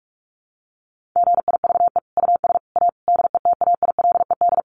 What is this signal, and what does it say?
Morse code, heard as a keyed tone, about 44 wpm, 700 Hz. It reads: GIVE US A DETAILED
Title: Sentences from Top 1500 Words 45wpm